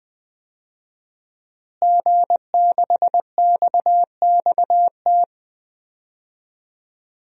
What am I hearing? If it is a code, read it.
Morse code, heard as a keyed tone, about 20 wpm, 700 Hz. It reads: G6XXT
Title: DXCC-100 Call Signs - 5 Characters - Repeated 2x - 20wpm